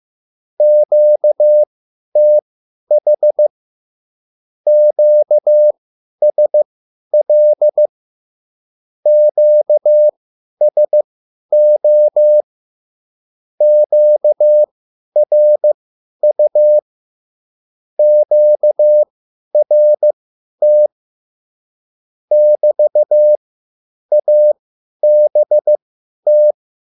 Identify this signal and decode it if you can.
Morse code, heard as a keyed tone, about 15 wpm, spaced as for 10 wpm, 600 Hz. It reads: QTH QSL QSO QRU QRT <BT>ABT